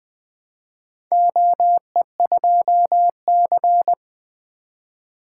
Morse code, heard as a keyed tone, about 20 wpm, 700 Hz. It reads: OE2C
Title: Four Letter-Number Combinations - 20wpm